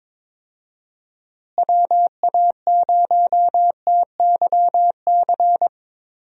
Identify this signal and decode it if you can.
Morse code, heard as a keyed tone, about 23 wpm, 700 Hz. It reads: WA0TYC
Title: USA Call Signs - 6 Characters - Repeated 3x - 22wpm